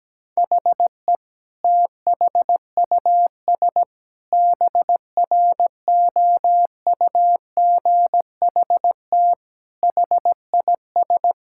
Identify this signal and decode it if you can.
Morse code, heard as a keyed tone, about 17 wpm, 700 Hz. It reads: HE THUS BROUGHT HIS
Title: Sentences from Top 1000 Words - Review Part 5 - 17wpm